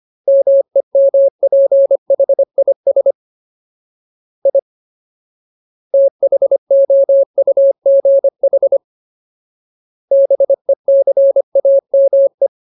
Morse code, heard as a keyed tone, about 25 wpm, 550 Hz. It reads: MEMPHIS I THOUGH BECAME
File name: Ultimate Word List / 25WPM Ultimate